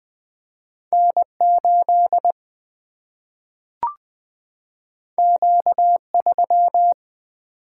Morse code, heard as a keyed tone, about 20 wpm, 700 Hz. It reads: N8 Q3
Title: Single-Letter Single-Number 20wpm